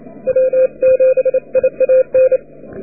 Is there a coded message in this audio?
Yes: Morse code from 0.3 to 2.4 s, heard as a keyed tone, about 32 wpm, 520 Hz. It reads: W7IAN